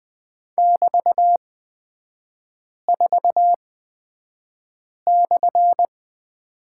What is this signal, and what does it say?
Morse code, heard as a keyed tone, about 20 wpm, spaced as for 5 wpm, 700 Hz. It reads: <BT>4/
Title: 224 Lesson 36e - BT - Three Characters 20wpm